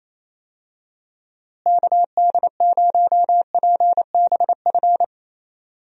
Morse code, heard as a keyed tone, about 28 wpm, 700 Hz. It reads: KD0PBF
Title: USA Call Signs - 6 Characters - Repeated 3x - 28wpm